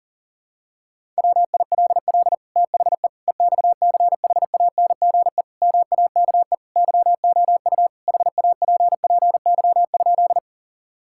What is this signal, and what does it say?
Morse code, heard as a keyed tone, about 40 wpm, 700 Hz. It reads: WILL THE EXCHANGE MAKE YOU HAPPY?
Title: Sentences from Top 1000 Words - Review Part 6 - 40wpm